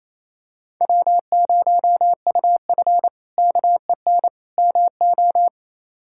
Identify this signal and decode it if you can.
Morse code, heard as a keyed tone, about 28 wpm, 700 Hz. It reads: W0UF KEN MO
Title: K1USN SST QSO Practice - 28wpm